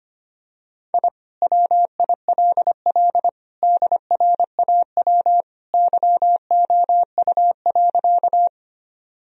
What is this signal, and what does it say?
Morse code, heard as a keyed tone, about 25 wpm, 700 Hz. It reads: I WILL DRAW YOU.
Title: Sentences from Top 1000 Words - Review Part 1 - 25wpm